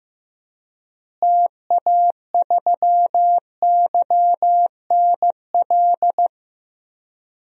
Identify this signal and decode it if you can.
Morse code, heard as a keyed tone, about 15 wpm, 700 Hz. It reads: TA3YNL
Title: DXCC-100 Call Signs - 6 Characters - Repeated 2x - 15wpm